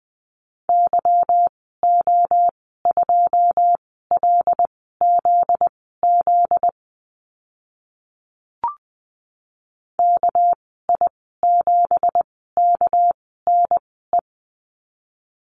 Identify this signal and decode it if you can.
Morse code, heard as a keyed tone, about 20 wpm, spaced as for 14 wpm, 700 Hz. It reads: YO2LZZ KI7KNE
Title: 245 Lesson 39g - Letter z - Call Signs 20wpm